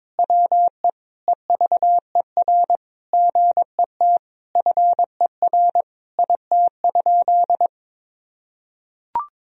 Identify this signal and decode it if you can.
Morse code, heard as a keyed tone, about 22 wpm, 700 Hz. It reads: WE EVER GET FER IT?
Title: Sentences from Top 400 Words and Abbreviations 22wpm